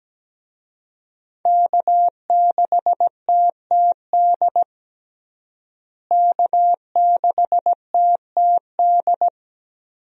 Morse code, heard as a keyed tone, about 17 wpm, 700 Hz. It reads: K6TTD K6TTD
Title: USA Call Signs - 5 Characters - Repeated 2x - 17wpm